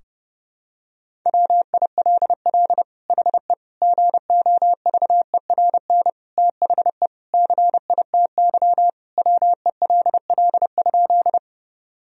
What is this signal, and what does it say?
Morse code, heard as a keyed tone, about 30 wpm, 700 Hz. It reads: WILL HE GOVERN THE CITY WELL?
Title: Sentences from Top 1000 Words - Review Part 2 - 30wpm